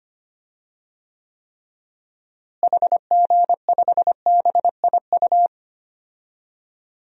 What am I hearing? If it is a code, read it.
Morse code, heard as a keyed tone, about 25 wpm, 700 Hz. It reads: HG5BIU
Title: DXCC-100 Call Signs - 6 Characters - Repeated 3x - 25wpm